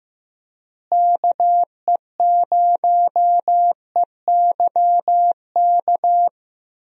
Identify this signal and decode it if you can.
Morse code, heard as a keyed tone, about 15 wpm, 700 Hz. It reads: KE0EYK
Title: USA Call Signs - 6 Characters - Repeated 3x - 15wpm